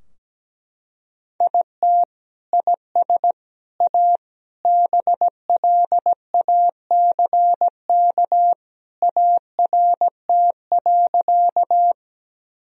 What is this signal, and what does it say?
Morse code, heard as a keyed tone, about 17 wpm, 700 Hz. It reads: IT IS A BLACK ART.
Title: Sentences from Top 1000 Words - Review Part 3 - 17wpm